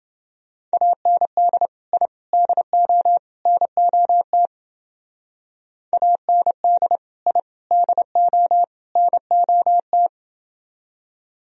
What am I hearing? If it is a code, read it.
Morse code, heard as a keyed tone, about 28 wpm, 700 Hz. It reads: AND I DO NOT AND I DO NOT